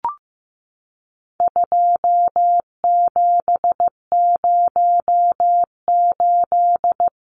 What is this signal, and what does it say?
Morse code, heard as a keyed tone, about 15 wpm, 700 Hz. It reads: 2708
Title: Numbers - 4-Digit 15wpm